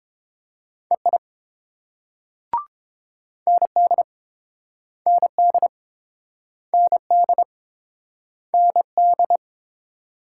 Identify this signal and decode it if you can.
Morse code, heard as a keyed tone, about 30 wpm, 700 Hz. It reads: EI ND ND ND ND